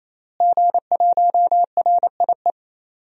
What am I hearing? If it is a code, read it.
Morse code, heard as a keyed tone, about 27 wpm, 700 Hz. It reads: G1RIE
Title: DXCC-100 Call Signs - 5 Characters - Repeated 3x - 28wpm